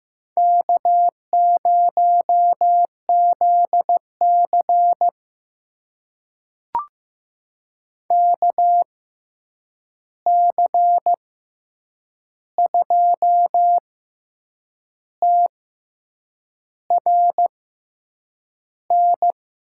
Morse code, heard as a keyed tone, about 15 wpm, spaced as for 7 wpm, 700 Hz. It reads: K0ZC KC2TRN
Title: USA Call Signs - 6x Character Spacing - 15wpm